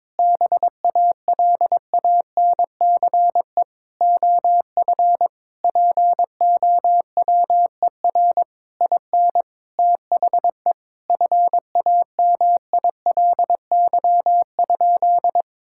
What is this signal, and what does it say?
Morse code, heard as a keyed tone, about 22 wpm, 700 Hz. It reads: BALANCE OF POWER IN THE FAMILY?
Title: Sentences from Top 1000 Words - Review Part 6 - 22wpm